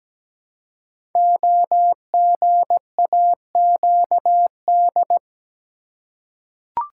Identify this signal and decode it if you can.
Morse code, heard as a keyed tone, about 17 wpm, 700 Hz. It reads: OGAQD
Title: Code Groups - 5 Letters - 17wpm